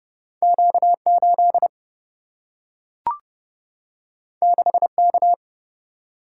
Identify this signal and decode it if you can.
Morse code, heard as a keyed tone, about 30 wpm, 700 Hz. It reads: Q8 6K
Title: Single-Letter Single-Number 30wpm